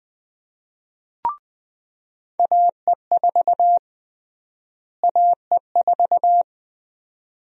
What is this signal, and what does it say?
Morse code, heard as a keyed tone, about 20 wpm, 700 Hz. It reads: AE4 AE4
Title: USA Call Sign Prefixes - Repeated 2x - 20wpm